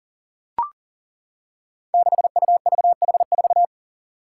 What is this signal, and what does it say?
Morse code, heard as a keyed tone, about 40 wpm, 700 Hz. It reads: 6UVH4
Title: Five Challenging Letter-Number Combinations 40wpm